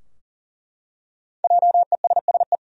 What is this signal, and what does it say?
Morse code, heard as a keyed tone, about 40 wpm, 700 Hz. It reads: JESSE
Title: USA Common Names - 40wpm